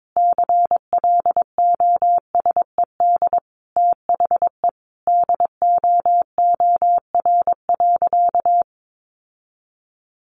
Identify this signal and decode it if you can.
Morse code, heard as a keyed tone, about 22 wpm, 700 Hz. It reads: CLOSED THE DOOR.
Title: Sentences from Top 1400 Words 22wpm